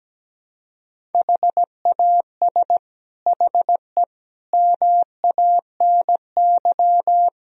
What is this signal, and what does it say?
Morse code, heard as a keyed tone, about 17 wpm, 700 Hz. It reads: HAS HE MANY
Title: Sentences from Top 1000 Words - Review Part 1 - 17wpm